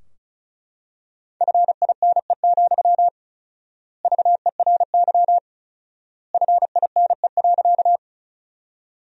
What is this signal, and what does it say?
Morse code, heard as a keyed tone, about 36 wpm, 700 Hz. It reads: FINE, VERY FINE.
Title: 4x Word-Spacing - Sentences from Top 500 Words - 35wpm